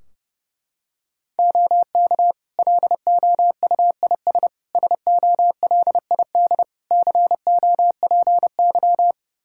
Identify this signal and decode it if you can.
Morse code, heard as a keyed tone, about 30 wpm, 700 Hz. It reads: OK LOUIS SOLID COPY
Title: Practice QSO - Exchange 3 - More Info - 30wpm